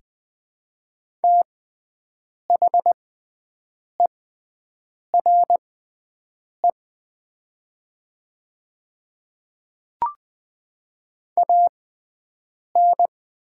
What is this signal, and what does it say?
Morse code, heard as a keyed tone, about 21 wpm, spaced as for 7 wpm, 700 Hz. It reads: THERE AN
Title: 6x Character Spacing - Top 100 Words - 20wpm